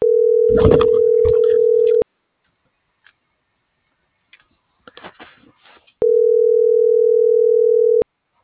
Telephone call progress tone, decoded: ringback tone

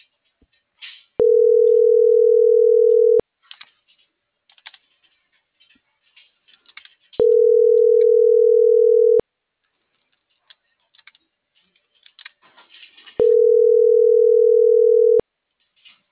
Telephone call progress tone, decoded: ringback tone

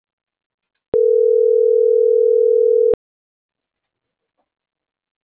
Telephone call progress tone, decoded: ringback tone

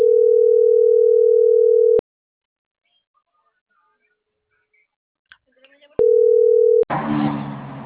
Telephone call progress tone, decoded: ringback tone

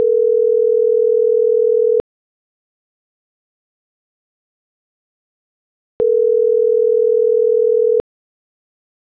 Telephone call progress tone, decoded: ringback tone

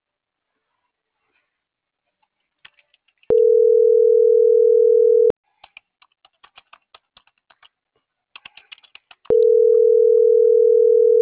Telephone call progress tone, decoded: ringback tone

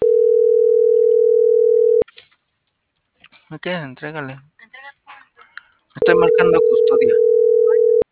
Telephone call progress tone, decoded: ringback tone

